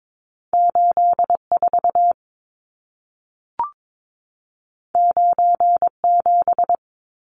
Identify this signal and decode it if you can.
Morse code, heard as a keyed tone, about 22 wpm, 700 Hz. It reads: 84 97
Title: Numbers - 2-Digit 22wpm